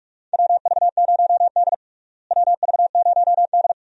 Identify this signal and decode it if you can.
Morse code, heard as a keyed tone, about 46 wpm, 700 Hz. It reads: WV0D WV0D
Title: USA Call Signs - 4 Characters - Repeated 2x - 45wpm